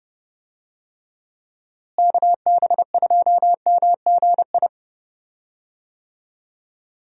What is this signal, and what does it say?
Morse code, heard as a keyed tone, about 30 wpm, 700 Hz. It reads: KB2MGI
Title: USA Call Signs - 6 Characters - Repeated 2x - 30wpm